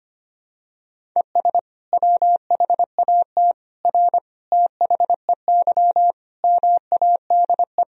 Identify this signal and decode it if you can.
Morse code, heard as a keyed tone, about 25 wpm, 700 Hz. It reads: ES WHAT R THEY MADE
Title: Sentences from Top 200 Words and Abbreviations 25wpm